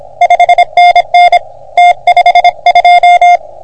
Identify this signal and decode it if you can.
Morse code, heard as a keyed tone, about 26 wpm, 720 Hz. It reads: 5NN T52
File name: OM3PA PILUP CUC 388